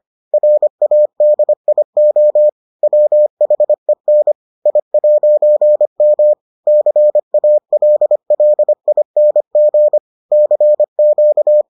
Morse code, heard as a keyed tone, about 25 wpm, 600 Hz. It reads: RADIO WHEN I'M CALLING CQ